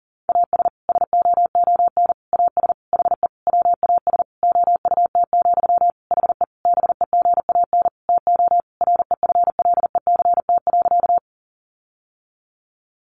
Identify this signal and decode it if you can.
Morse code, heard as a keyed tone, about 40 wpm, 700 Hz. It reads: AS SOON AS HE WAS OUT, HE BEGAN TO REFLECT.
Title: Sentences from Top 1500 Words 40wpm